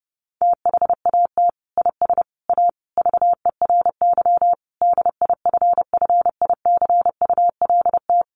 Morse code, heard as a keyed tone, about 30 wpm, 700 Hz. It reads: THAT IS A VERY DIFFICULT
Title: Sentences from Top 1200 Words 30wpm